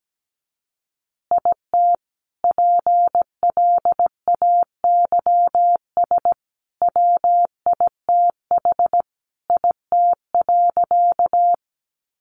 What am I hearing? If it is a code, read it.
Morse code, heard as a keyed tone, about 17 wpm, 700 Hz. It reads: IT PLAYS WITH IT.
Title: Additional Easy Phrases 17wpm